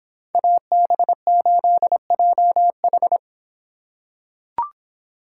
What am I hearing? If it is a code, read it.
Morse code, heard as a keyed tone, about 25 wpm, 700 Hz. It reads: AB8JH